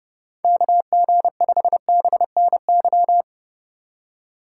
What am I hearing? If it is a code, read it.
Morse code, heard as a keyed tone, about 30 wpm, 700 Hz. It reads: KG5BNY